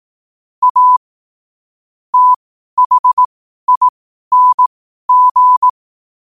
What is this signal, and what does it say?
Morse code, heard as a keyed tone, about 18 wpm, spaced as for 12 wpm, 1000 Hz. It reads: A THING
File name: qotdfiles / 12wpm